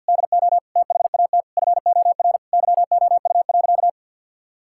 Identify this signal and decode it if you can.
Morse code, heard as a keyed tone, about 50 wpm, 700 Hz. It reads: DO THAT FOR YOU.
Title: Sentences from Top 1000 Words - Review Part 6 - 50wpm